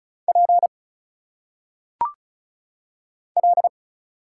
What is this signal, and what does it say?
Morse code, heard as a keyed tone, about 34 wpm, 700 Hz. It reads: P L
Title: Single Letters - Phonetic Pronunciation 35wpm